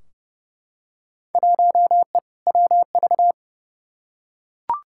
Morse code, heard as a keyed tone, about 30 wpm, 700 Hz. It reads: 1E WV
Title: Field Day Exchanges Encore - 30wpm